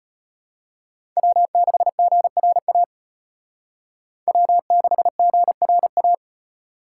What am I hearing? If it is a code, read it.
Morse code, heard as a keyed tone, about 32 wpm, 700 Hz. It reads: W6GRA W6GRA